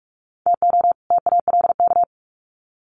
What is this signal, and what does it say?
Morse code, heard as a keyed tone, about 44 wpm, 700 Hz. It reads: TO TALK
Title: Sets of 2 Words 45wpm